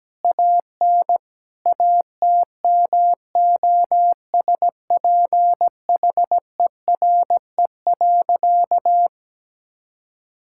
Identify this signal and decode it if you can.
Morse code, heard as a keyed tone, about 17 wpm, 700 Hz. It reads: AN ATMOSPHERE.
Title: Sentences from Top 1600 Words 17wpm